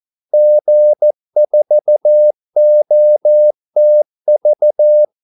Morse code, heard as a keyed tone, about 14 wpm, 600 Hz. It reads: G4OTV
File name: Callsigns-14wpm